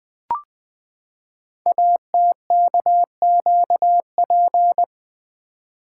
Morse code, heard as a keyed tone, about 20 wpm, 700 Hz. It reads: ATKQP
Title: Code Groups - 5 Letters - 20wpm